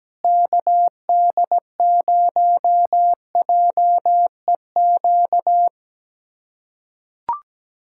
Morse code, heard as a keyed tone, about 17 wpm, 700 Hz. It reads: KD0JEQ
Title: USA Callsign DE Callsign 17wpm